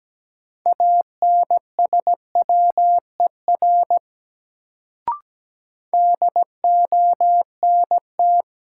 Morse code, heard as a keyed tone, about 17 wpm, 700 Hz. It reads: ANSWER DONT
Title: Sentences from Top 300 Words - New Word Drill - 17wpm